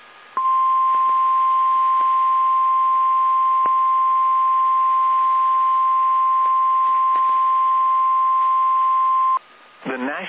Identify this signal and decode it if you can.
Emergency Alert System (SAME) tones (the 1050 Hz attention tone). It is heard from 0.4 to 9.4 s.
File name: eas / KSDZKDJL / KSDZKDJL